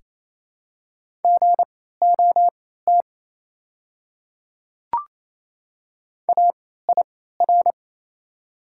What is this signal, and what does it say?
Morse code, heard as a keyed tone, about 28 wpm, spaced as for 16 wpm, 700 Hz. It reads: GOT AIR
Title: 3x Character Spacing - Top 300 Words - 28wpm